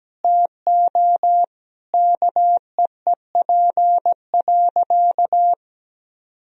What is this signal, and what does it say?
Morse code, heard as a keyed tone, about 17 wpm, 700 Hz. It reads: TO KEEP.
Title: Sentences from Top 1000 Words - Review Part 2 - 17wpm